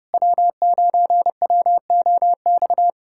Morse code, heard as a keyed tone, about 30 wpm, 700 Hz. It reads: W9WOX
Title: USA Callsign DE Callsign - 5 Characters - 30wpm